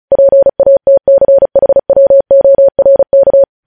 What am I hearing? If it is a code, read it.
Morse code, heard as a keyed tone, about 36 wpm, 550 Hz. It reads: PATCHWORK
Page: News Headlines in Morse Code at 35 WPM